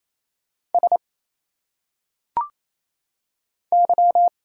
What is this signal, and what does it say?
Morse code, heard as a keyed tone, about 27 wpm, 700 Hz. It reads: S Y
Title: Single Letters - Phonetic Pronunciation 28wpm